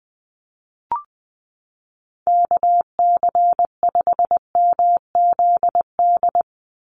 Morse code, heard as a keyed tone, about 20 wpm, 700 Hz. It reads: KC5MZD
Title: USA Call Signs 20wpm